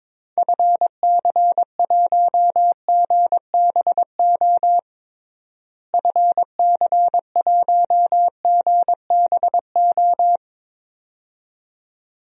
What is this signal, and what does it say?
Morse code, heard as a keyed tone, about 22 wpm, 700 Hz. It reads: FC1GBO FC1GBO
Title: DXCC-100 Call Signs - 6 Characters - Repeated 2x - 22wpm